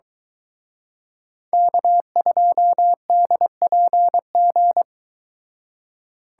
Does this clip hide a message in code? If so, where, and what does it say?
Yes: Morse code from 1.5 to 4.8 s, heard as a keyed tone, about 23 wpm, 700 Hz. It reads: K2DPG